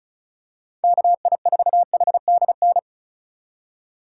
Morse code, heard as a keyed tone, about 36 wpm, 700 Hz. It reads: KI4HDN
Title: USA Call Signs - 6 Characters - 6x Character Spacing - 35wpm